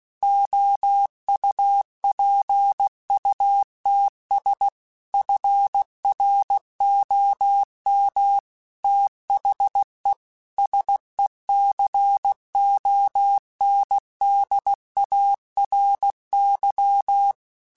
Morse code, heard as a keyed tone, about 16 wpm, 800 Hz. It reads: OUPUTS FROM THE SECONDARY
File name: Technical Subjects / Technical Subjects-mp3s / Sw Pwr Sup @15